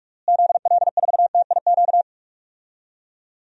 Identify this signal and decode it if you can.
Morse code, heard as a keyed tone, about 46 wpm, 700 Hz. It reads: ZL4TIQ
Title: DXCC-100 Call Signs - 6 Characters - Repeated 3x - 45wpm